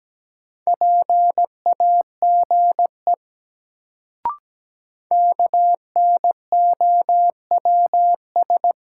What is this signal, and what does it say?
Morse code, heard as a keyed tone, about 17 wpm, 700 Hz. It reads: PAGE KNOWS
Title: Sentences from Top 1000 Words - Review Part 1 - New Word Drill - 17wpm